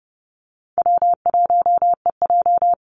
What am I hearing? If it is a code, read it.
Morse code, heard as a keyed tone, about 30 wpm, 700 Hz. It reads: W1EJ
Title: NA SOTA Activator Callsigns 30wpm